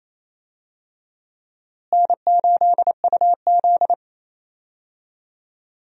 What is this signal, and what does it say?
Morse code, heard as a keyed tone, about 28 wpm, 700 Hz. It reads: N8UZ
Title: USA Call Signs - 4 Characters - Repeated 3x - 28wpm